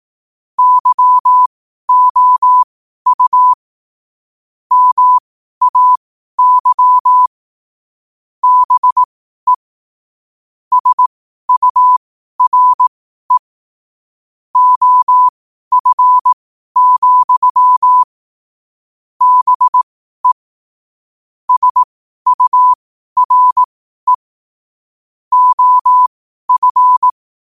Morse code, heard as a keyed tone, about 18 wpm, spaced as for 12 wpm, 1000 Hz. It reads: YOU MAY BE SURE OF, BE SURE OF
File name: qotdfiles / 12wpm